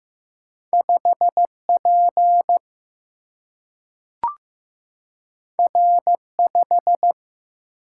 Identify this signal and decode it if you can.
Morse code, heard as a keyed tone, about 15 wpm, 700 Hz. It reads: 5P R5
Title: Single-Letter Single-Number 15wpm